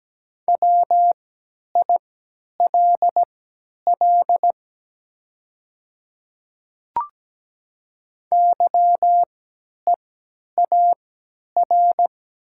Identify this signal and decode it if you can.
Morse code, heard as a keyed tone, about 17 wpm, spaced as for 10 wpm, 700 Hz. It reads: WILL YEAR
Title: 3x Character Spacing - Top 300 Words - 17wpm